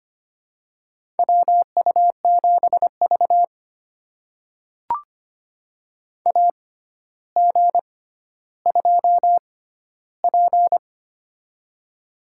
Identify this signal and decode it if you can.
Morse code, heard as a keyed tone, about 25 wpm, spaced as for 13 wpm, 700 Hz. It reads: WU7V AG2P
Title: USA Call Signs - 4 Characters - 6x Character Spacing - 25wpm